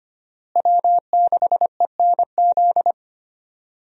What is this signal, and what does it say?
Morse code, heard as a keyed tone, about 25 wpm, 700 Hz. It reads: W6ENZ
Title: USA Call Signs - 5 Characters - Repeated 3x - 25wpm